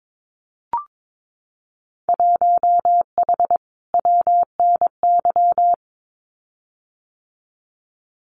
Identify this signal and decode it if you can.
Morse code, heard as a keyed tone, about 22 wpm, 700 Hz. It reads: 1H WNY
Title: Winter Field Day Exchanges 22wpm